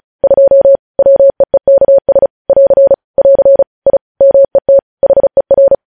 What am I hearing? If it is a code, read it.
Morse code, heard as a keyed tone, about 36 wpm, 550 Hz. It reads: 2 WEEKS <AR> <AR> I MET HER
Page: News headlines in Morse code at 35 wpm